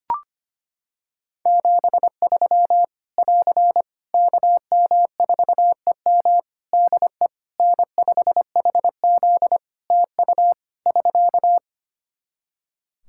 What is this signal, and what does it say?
Morse code, heard as a keyed tone, about 25 wpm, 700 Hz. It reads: 73 <AR> KM4EM DE N5HZ TU <SK>
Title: Practice QSO - Exchange 4 - Ending - 25wpm